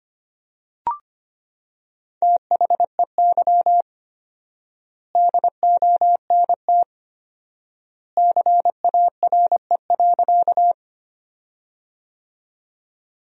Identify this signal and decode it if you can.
Morse code, heard as a keyed tone, about 25 wpm, 700 Hz. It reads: THEY DONT CARE.
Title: 4x Word-Spacing - Sentences from Top 300 Words - 25wpm